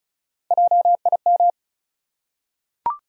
Morse code, heard as a keyed tone, about 35 wpm, 700 Hz. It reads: JIM
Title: USA Common Names - Max 4 Letters - 35wpm